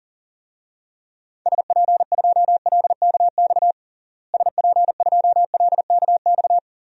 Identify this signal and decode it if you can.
Morse code, heard as a keyed tone, about 40 wpm, 700 Hz. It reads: SP2LKX SP2LKX
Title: DXCC-100 Call Signs - 6 Characters - Repeated 3x - 40wpm